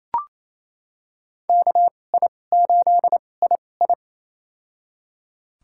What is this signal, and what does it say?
Morse code, heard as a keyed tone, about 28 wpm, spaced as for 20 wpm, 700 Hz. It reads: KI8II
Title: USA Call Signs - 5 Characters - 2x Character Spacing - 28wpm